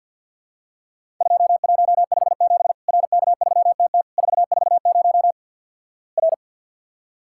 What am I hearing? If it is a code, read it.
Morse code, heard as a keyed tone, about 50 wpm, 700 Hz. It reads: 2157 RK3TT 440 R
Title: Worked All Europe DX Contest - QTC Exchange - 50wpm